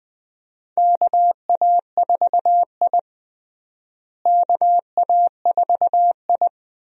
Morse code, heard as a keyed tone, about 21 wpm, 700 Hz. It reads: KA4I KA4I
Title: USA Call Signs - 4 Characters - Repeated 3x - 20wpm